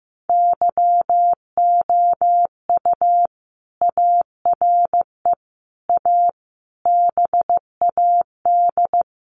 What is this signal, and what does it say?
Morse code, heard as a keyed tone, about 15 wpm, 700 Hz. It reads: YOU ARE A BAD
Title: Sentences from Top 700 Words 15wpm